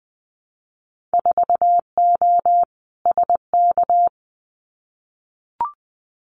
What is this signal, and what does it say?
Morse code, heard as a keyed tone, about 20 wpm, 700 Hz. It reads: 4O SK
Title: Winter Field Day Exchanges Encore 20wpm